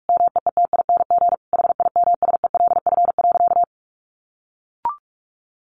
Morse code, heard as a keyed tone, about 44 wpm, 700 Hz. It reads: MEETING HIMSELF.
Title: A Sound Of Thunder 45wpm